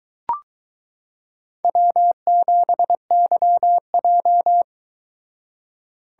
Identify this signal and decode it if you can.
Morse code, heard as a keyed tone, about 22 wpm, 700 Hz. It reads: W7YJ